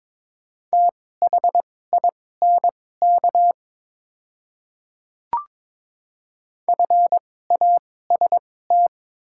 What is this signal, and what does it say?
Morse code, heard as a keyed tone, about 22 wpm, spaced as for 16 wpm, 690 Hz. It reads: THINK FAST
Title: 2x Character Spacing - Top 500 Words - 22wpm